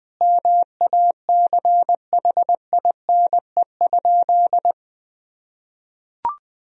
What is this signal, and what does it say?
Morse code, heard as a keyed tone, about 20 wpm, 700 Hz. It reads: MACHINE?
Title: Sentences from Top 1000 Words - Review Part 2 - 20wpm